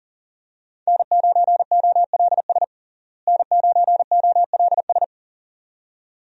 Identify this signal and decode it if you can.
Morse code, heard as a keyed tone, about 40 wpm, 700 Hz. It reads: N9OLS N9OLS
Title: USA Call Signs - 5 Characters - Repeated 2x - 40wpm